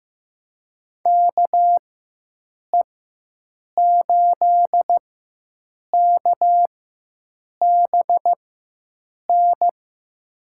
Morse code, heard as a keyed tone, about 15 wpm, spaced as for 7 wpm, 700 Hz. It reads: KE8KBN
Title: USA Call Signs - 6 Characters - 4x Character Spacing - 15wpm